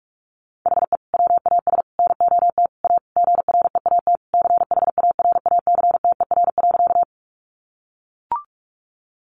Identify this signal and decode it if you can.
Morse code, heard as a keyed tone, about 44 wpm, 700 Hz. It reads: HE WAS NOT A GREAT CHARACTER.
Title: Sentences from Top 1200 Words 45wpm